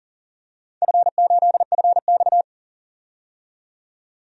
Morse code, heard as a keyed tone, about 42 wpm, 700 Hz. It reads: F8FX
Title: DXCC-100 Call Signs - 4 Characters - 40wpm